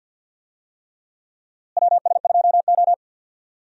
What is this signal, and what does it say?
Morse code, heard as a keyed tone, about 53 wpm, 700 Hz. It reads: WS2O
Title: USA Call Signs - 4 Characters - Repeated 3x - 50wpm